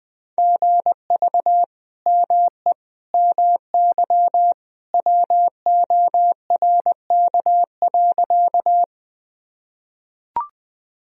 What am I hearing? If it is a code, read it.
Morse code, heard as a keyed tone, about 21 wpm, 700 Hz. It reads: GV ME MY WORK.
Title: Sentences from Top 200 Words and Abbreviations 20wpm